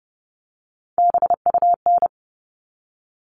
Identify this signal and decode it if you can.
Morse code, heard as a keyed tone, about 30 wpm, 700 Hz. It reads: BUN
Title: All Cut Numbers - 3-Digit 30wpm